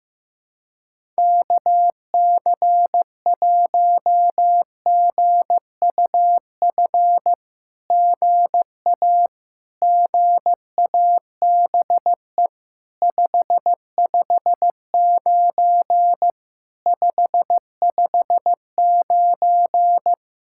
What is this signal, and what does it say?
Morse code, heard as a keyed tone, about 14 wpm, 700 Hz. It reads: KC1GUF GA GABE 559 559
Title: SOTA QSO Practice 15wpm